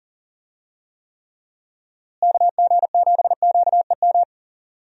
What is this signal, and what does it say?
Morse code, heard as a keyed tone, about 40 wpm, 700 Hz. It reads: KG7QEM
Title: USA Call Signs - 6 Characters - Repeated 3x - 40wpm